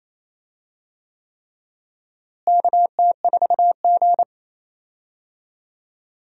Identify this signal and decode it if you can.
Morse code, heard as a keyed tone, about 28 wpm, 700 Hz. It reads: KT4G
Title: USA Call Signs - 4 Characters - Repeated 3x - 28wpm